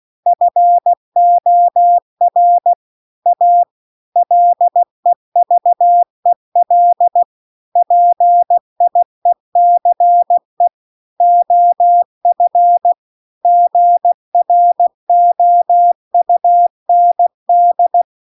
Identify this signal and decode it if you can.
Morse code, heard as a keyed tone, about 16 wpm, 700 Hz. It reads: FOR A LEVEL PIECE OF GROUND